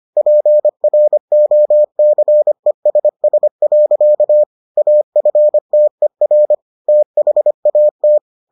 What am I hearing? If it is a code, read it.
Morse code, heard as a keyed tone, about 25 wpm, 600 Hz. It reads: PROCESS. AFTER THAT